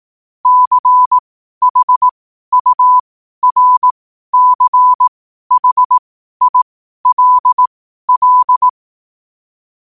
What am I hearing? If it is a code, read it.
Morse code, heard as a keyed tone, about 18 wpm, spaced as for 12 wpm, 1000 Hz. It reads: CHURCHILL
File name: qotdfiles / 12wpm